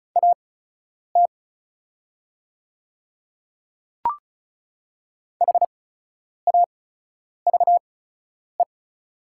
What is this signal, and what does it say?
Morse code, heard as a keyed tone, about 34 wpm, spaced as for 10 wpm, 690 Hz. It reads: AT HAVE
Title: 8x Character Spacing - Top 200 Words - 35wpm